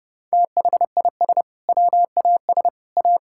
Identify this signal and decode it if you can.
Morse code, heard as a keyed tone, about 32 wpm, 700 Hz. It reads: THIS WAS A
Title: Sentences from Top 1000 Words - Review Part 5 - 30wpm